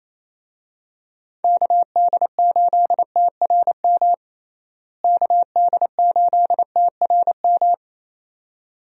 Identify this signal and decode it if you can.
Morse code, heard as a keyed tone, about 28 wpm, 700 Hz. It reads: KD8TRM KD8TRM
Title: USA Call Signs - 6 Characters - Repeated 2x - 28wpm